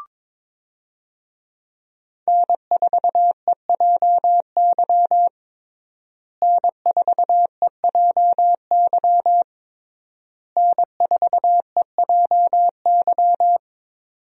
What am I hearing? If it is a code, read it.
Morse code, heard as a keyed tone, about 21 wpm, 700 Hz. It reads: N4EJY N4EJY N4EJY
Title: USA Call Signs - 5 Characters - Repeated 3x - 22wpm